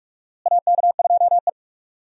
Morse code, heard as a keyed tone, about 46 wpm, 700 Hz. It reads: AK2E
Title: USA Call Signs - 4 Characters - Repeated 2x - 45wpm